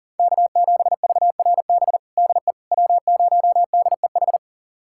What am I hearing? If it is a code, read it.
Morse code, heard as a keyed tone, about 40 wpm, 700 Hz. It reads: K7VRB DE W0DEH
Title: USA Callsign DE Callsign - 5 Characters - 40wpm